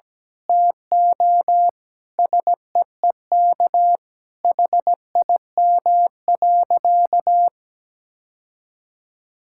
Morse code, heard as a keyed tone, about 17 wpm, 700 Hz. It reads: TO SEEK HIM.
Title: Sentences from Top 1600 Words 17wpm